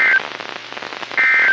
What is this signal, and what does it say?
Emergency Alert System (SAME) tones (end-of-message bursts), heard as harsh buzzing bursts. It is heard from 1.2 to 1.5 s.
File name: eas messages / EAS WXL38 Binghamton